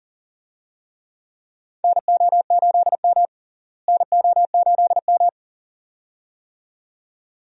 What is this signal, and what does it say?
Morse code, heard as a keyed tone, about 40 wpm, 700 Hz. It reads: NO8M NO8M
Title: USA Call Signs - 4 Characters - Repeated 2x - 40wpm